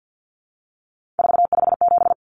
Morse code, heard as a keyed tone, about 50 wpm, 700 Hz. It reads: 457
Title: Common Cut Numbers - 3-Digit 50wpm